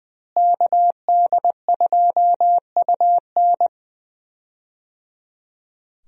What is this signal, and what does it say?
Morse code, heard as a keyed tone, about 20 wpm, 700 Hz. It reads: KD2UN
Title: USA Call Signs - 5 Characters - Repeated 3x - 20wpm